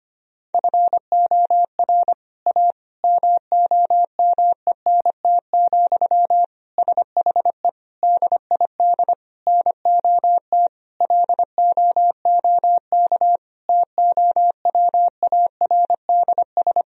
Morse code, heard as a keyed tone, about 25 wpm, 700 Hz. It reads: FOR A MOMENT, SHE DID NOT LOOK TOWARDS
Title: Sentences from Top 1000 Words - Review Part 2 - 25wpm